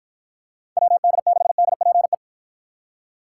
Morse code, heard as a keyed tone, about 53 wpm, 700 Hz. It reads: WD6DPE